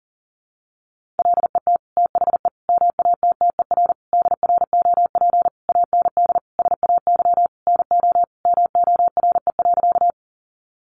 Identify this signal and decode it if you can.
Morse code, heard as a keyed tone, about 40 wpm, 700 Hz. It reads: LET THE MATTER DROP AND SAY NO MORE.
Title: Sentences from Top 1400 Words 40wpm